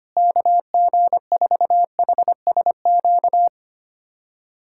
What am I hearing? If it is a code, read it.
Morse code, heard as a keyed tone, about 25 wpm, 700 Hz. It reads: KG4HSQ
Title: USA Callsign DE Callsign 25wpm